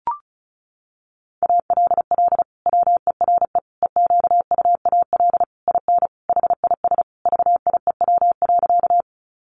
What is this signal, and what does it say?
Morse code, heard as a keyed tone, about 34 wpm, 700 Hz. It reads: ALL WERE EQUAL IN HIS VIEW.
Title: Sentences from Top 1400 Words 35wpm